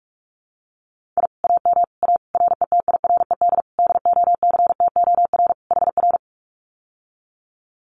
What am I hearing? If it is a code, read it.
Morse code, heard as a keyed tone, about 46 wpm, 700 Hz. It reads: I AM A RETIRED DOCTOR HR
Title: Common QSO Phrases Encore 45wpm